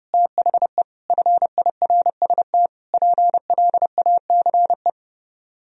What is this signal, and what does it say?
Morse code, heard as a keyed tone, about 38 wpm, 700 Hz. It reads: THE FIRST PLACE